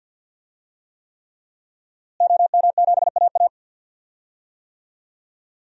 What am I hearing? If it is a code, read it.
Morse code, heard as a keyed tone, about 52 wpm, 700 Hz. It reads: OM7AA
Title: DXCC-100 Call Signs - 5 Characters - Repeated 3x - 50wpm